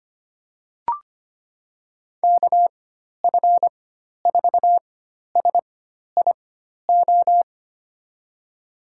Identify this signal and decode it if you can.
Morse code, heard as a keyed tone, about 25 wpm, spaced as for 12 wpm, 700 Hz. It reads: KF4SIO
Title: USA Call Signs - 6 Characters - 4x Character Spacing - 25wpm